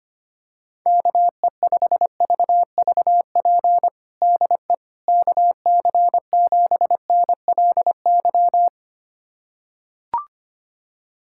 Morse code, heard as a keyed tone, about 25 wpm, 700 Hz. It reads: KE5VVP DE KC7NLY
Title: USA Callsign DE Callsign 25wpm